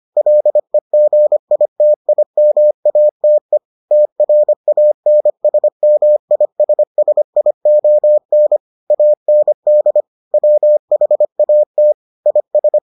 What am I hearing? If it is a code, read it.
Morse code, heard as a keyed tone, about 25 wpm, 600 Hz. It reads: LEGITIMATE TRANSMISSION AND WHAT IS